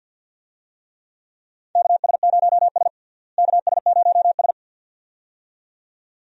Morse code, heard as a keyed tone, about 46 wpm, 700 Hz. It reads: KS0S KS0S
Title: USA Call Signs - 4 Characters - Repeated 2x - 50wpm